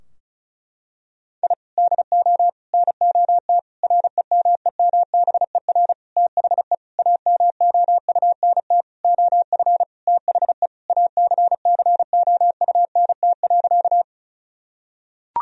Morse code, heard as a keyed tone, about 34 wpm, 700 Hz. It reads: I DO NOT REMEMBER THE AMOUNT OF THE ACCOUNT.
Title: Sentences from Top 1000 Words - Review Part 4 - 35wpm